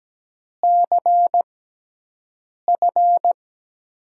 Morse code, heard as a keyed tone, about 17 wpm, spaced as for 6 wpm, 700 Hz. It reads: CF
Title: USA Call Signs - 5 Characters - 6x Character Spacing - 17wpm